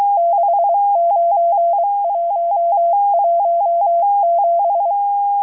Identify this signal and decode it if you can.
Morse code, heard as a keyed tone, about 23 wpm, 680 Hz, with radio noise. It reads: 69117